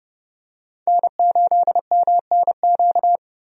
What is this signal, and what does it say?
Morse code, heard as a keyed tone, about 32 wpm, 700 Hz. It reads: N8MNQ
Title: USA Call Signs - 5 Characters - 30wpm